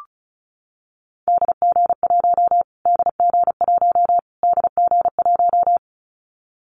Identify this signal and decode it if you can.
Morse code, heard as a keyed tone, about 34 wpm, 700 Hz. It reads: DG1 DG1 DG1
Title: Callsign Prefixes for Top 25 DXCC - Repeated 3x - 35wpm